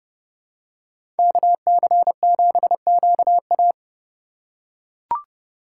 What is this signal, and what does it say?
Morse code, heard as a keyed tone, about 31 wpm, 700 Hz. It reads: KC7QA
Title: USA Call Signs - 5 Characters - 3x Character Spacing - 30wpm